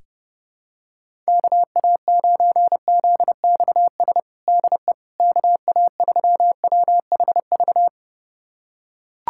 Morse code, heard as a keyed tone, about 30 wpm, 700 Hz. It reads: KA9ZXS DE KA3WHV
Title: USA Callsign DE Callsign - 6 Characters - 30wpm